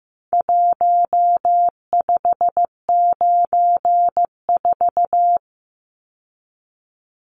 Morse code, heard as a keyed tone, about 15 wpm, 700 Hz. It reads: 1594
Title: Numbers - Contest Style - 4-Digit 15wpm